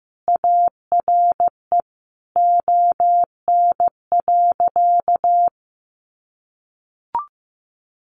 Morse code, heard as a keyed tone, about 15 wpm, 700 Hz. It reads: ARE ON.
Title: Sentences from Top 1100 Words 15wpm